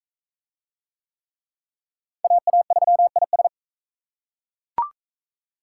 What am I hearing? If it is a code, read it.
Morse code, heard as a keyed tone, about 37 wpm, 700 Hz. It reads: AA3IS